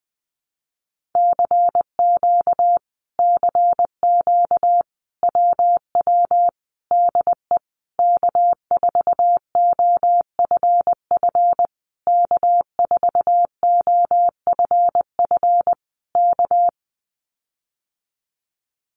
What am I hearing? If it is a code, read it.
Morse code, heard as a keyed tone, about 20 wpm, 700 Hz. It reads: CQ CQ WW DE K4OFF K4OFF K
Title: Common QSO Phrases Encore 20wpm